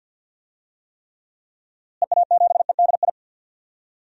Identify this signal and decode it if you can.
Morse code, heard as a keyed tone, about 49 wpm, 700 Hz. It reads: EA7EDI
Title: DXCC-100 Call Signs - 6 Characters - Repeated 2x - 50wpm